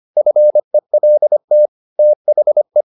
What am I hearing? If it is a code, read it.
Morse code, heard as a keyed tone, about 25 wpm, 600 Hz. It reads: FELT THE